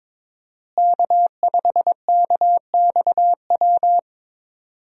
Morse code, heard as a keyed tone, about 21 wpm, 700 Hz. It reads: K5KXW
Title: USA Call Signs - 5 Characters - Repeated 3x - 22wpm